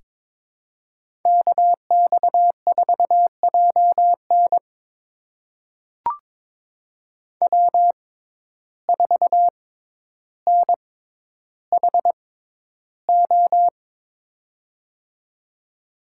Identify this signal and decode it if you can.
Morse code, heard as a keyed tone, about 23 wpm, spaced as for 11 wpm, 700 Hz. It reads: KX4JN W4NHO
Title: USA Call Signs - 5 Characters - 6x Character Spacing - 22wpm